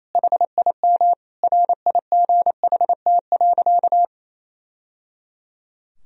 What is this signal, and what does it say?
Morse code, heard as a keyed tone, about 28 wpm, 700 Hz. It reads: HIM RIGHT.
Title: Sentences from Top 1000 Words - Review Part 2 - 28wpm